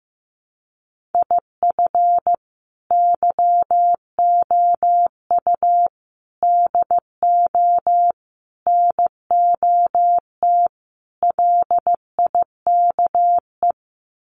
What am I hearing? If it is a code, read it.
Morse code, heard as a keyed tone, about 15 wpm, 700 Hz. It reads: IF YOU DO NOT LIKE
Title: Sets of 5 Words 15wpm